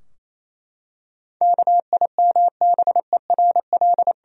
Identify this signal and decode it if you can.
Morse code, heard as a keyed tone, about 28 wpm, 700 Hz. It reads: KIMBERL
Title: USA Common Names - 28wpm